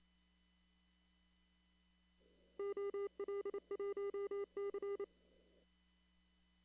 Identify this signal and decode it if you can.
Morse code, heard as a keyed tone, about 28 wpm, 400 Hz. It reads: OL1C
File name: CW / kt5j wpx cw-op-k5pi